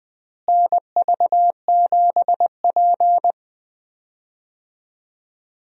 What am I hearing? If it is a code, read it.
Morse code, heard as a keyed tone, about 21 wpm, 700 Hz. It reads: NV7P
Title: USA Call Signs - 4 Characters - Repeated 3x - 20wpm